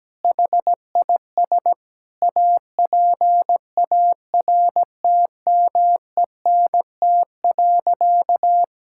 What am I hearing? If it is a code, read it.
Morse code, heard as a keyed tone, about 17 wpm, 700 Hz. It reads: HIS APARTMENT.
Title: Sentences from Top 1000 Words - Review Part 6 - 17wpm